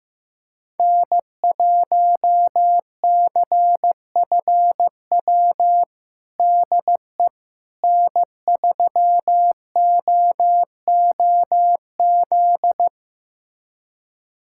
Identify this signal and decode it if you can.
Morse code, heard as a keyed tone, about 16 wpm, 700 Hz. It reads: N1CFW DE N3OOZ
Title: USA Callsign DE Callsign 15wpm